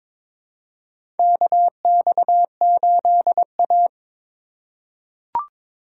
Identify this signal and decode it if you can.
Morse code, heard as a keyed tone, about 22 wpm, 700 Hz. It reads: KX8A
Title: USA Call Signs - 4 Characters - 3x Character Spacing - 22wpm